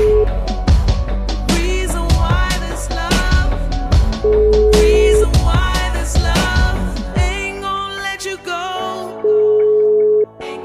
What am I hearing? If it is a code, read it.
Ringback tone (Europe, 425 Hz), with noise.